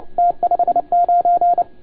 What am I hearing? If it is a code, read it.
Morse code, heard as a keyed tone, about 29 wpm, 690 Hz. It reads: T59
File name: Archiv / CUC287 / OK1FGD CUC 287